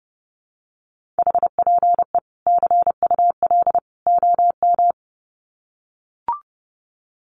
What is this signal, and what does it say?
Morse code, heard as a keyed tone, about 30 wpm, 700 Hz. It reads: HPE CUL OM
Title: Common QSO Phrases 30wpm